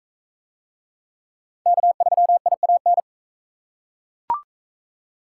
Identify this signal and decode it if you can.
Morse code, heard as a keyed tone, about 40 wpm, 700 Hz. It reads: K3IAN